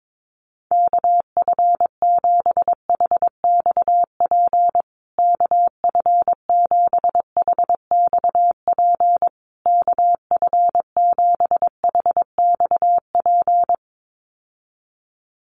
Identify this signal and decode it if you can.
Morse code, heard as a keyed tone, about 22 wpm, 700 Hz. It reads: KF7HXP KF7HXP KF7HXP
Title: USA Call Signs Repeated 3x 22wpm